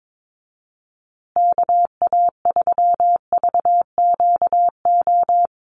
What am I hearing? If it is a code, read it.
Morse code, heard as a keyed tone, about 22 wpm, 700 Hz. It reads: KA3VQO
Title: USA Call Signs Repeated 2x 22wpm